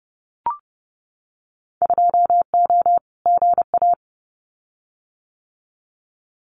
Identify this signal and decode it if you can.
Morse code, heard as a keyed tone, about 30 wpm, 700 Hz. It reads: 2O GA
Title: Winter Field Day Exchanges Encore 30wpm